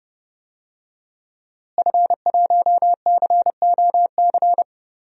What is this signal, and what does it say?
Morse code, heard as a keyed tone, about 30 wpm, 700 Hz. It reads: F1COC
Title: DXCC-100 Call Signs - 5 Characters - Repeated 2x - 30wpm